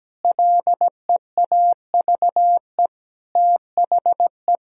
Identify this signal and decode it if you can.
Morse code, heard as a keyed tone, about 17 wpm, 700 Hz. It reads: LEAVE THE
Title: Sentences from Top 1000 Words - Review Part 1 - 17wpm